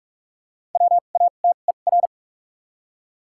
Morse code, heard as a keyed tone, about 44 wpm, spaced as for 32 wpm, 700 Hz. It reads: WATER
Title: 2x Character Spacing - Top 100 Words - 45wpm